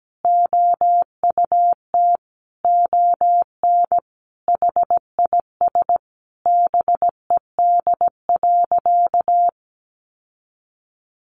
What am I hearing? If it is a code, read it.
Morse code, heard as a keyed tone, about 17 wpm, 700 Hz. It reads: OUT ON HIS BED.
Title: Sentences from Top 1200 Words 17wpm